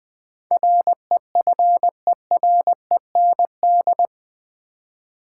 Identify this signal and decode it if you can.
Morse code, heard as a keyed tone, about 20 wpm, 700 Hz. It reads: REFEREND
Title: Build A Word - 4 Letters At A Time - 10-Letter Words - 20wpm